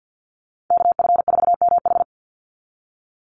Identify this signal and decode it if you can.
Morse code, heard as a keyed tone, about 49 wpm, 700 Hz. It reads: KF4MH
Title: NA SOTA Chaser Callsigns 50wpm